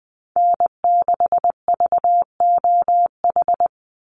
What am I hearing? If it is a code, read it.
Morse code, heard as a keyed tone, about 20 wpm, 700 Hz. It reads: N6VOH
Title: Call Signs Repeated 3x - 20wpm